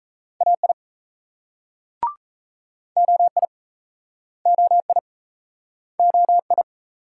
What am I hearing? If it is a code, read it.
Morse code, heard as a keyed tone, about 39 wpm, 700 Hz. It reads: AI OI OI OI